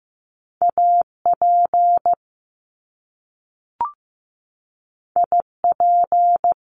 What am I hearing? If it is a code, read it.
Morse code, heard as a keyed tone, about 15 wpm, 700 Hz. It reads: AP IP
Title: Top Two Letter Combinations in Words 15wpm